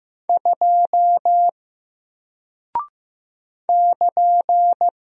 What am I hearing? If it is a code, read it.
Morse code, heard as a keyed tone, about 15 wpm, 700 Hz. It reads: 2 <KN>
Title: Single Character (Letter:Number:Punctuation:Prosign) Alt 15wpm